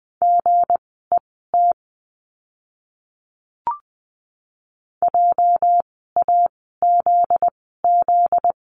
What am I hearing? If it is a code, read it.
Morse code, heard as a keyed tone, about 20 wpm, spaced as for 15 wpm, 700 Hz. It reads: GET JAZZ
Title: 244 Lesson 39f - Letter z - Words 20wpm